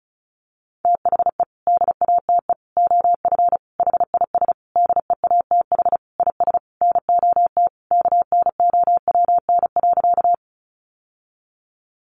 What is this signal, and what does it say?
Morse code, heard as a keyed tone, about 34 wpm, 700 Hz. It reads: THE DATE OF HIS DEATH IS NOT KNOWN.
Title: Sentences from Top 1100 Words 35wpm